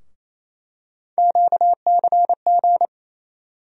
Morse code, heard as a keyed tone, about 28 wpm, 700 Hz. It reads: QCG
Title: USA Call Sign Postfixes - 3 Letters 28wpm